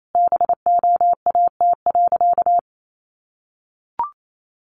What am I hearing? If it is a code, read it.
Morse code, heard as a keyed tone, about 28 wpm, 700 Hz. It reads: BOAT.
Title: Sentences from Top 1500 Words 28wpm